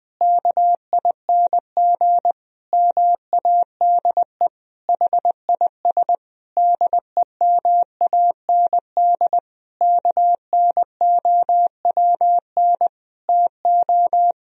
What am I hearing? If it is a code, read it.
Morse code, heard as a keyed tone, about 20 wpm, 700 Hz. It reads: KING MADE HIS DEMAND KNOWN TO
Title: Sentences from Top 1000 Words - Review Part 4 - 20wpm